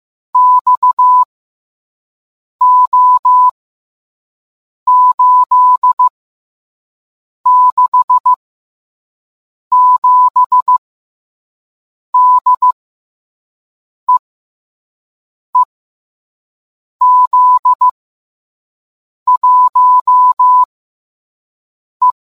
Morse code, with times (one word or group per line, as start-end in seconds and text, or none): 0.3-1.2 X
2.6-3.5 O
4.9-6.1 8
7.5-8.3 6
9.7-10.8 7
12.1-12.7 D
14.1-14.2 E
15.6-15.6 E
17.0-17.9 Z
19.3-20.6 1
22.0-22.1 E